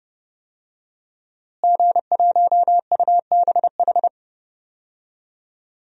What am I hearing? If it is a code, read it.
Morse code, heard as a keyed tone, about 30 wpm, 700 Hz. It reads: G1UBH
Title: DXCC-100 Call Signs - 5 Characters - Repeated 3x - 30wpm